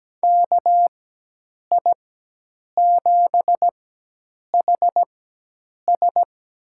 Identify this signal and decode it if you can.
Morse code, heard as a keyed tone, about 17 wpm, spaced as for 8 wpm, 700 Hz. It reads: KI7HS
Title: USA Call Signs - 4x Character Spacing - 17wpm